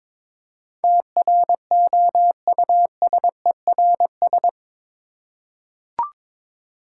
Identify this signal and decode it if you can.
Morse code, heard as a keyed tone, about 22 wpm, 690 Hz. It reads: TROUSERS
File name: Difficult Words with Hints - 22wpm